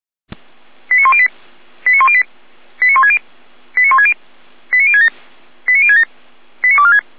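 Five-tone selective call (ZVEI), heard as quick runs of five tones, tones of about 70 ms. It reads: 89108 89108 89160 89160 89067 89067 89246
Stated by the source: ZVEI five-tone selective call received by SDR.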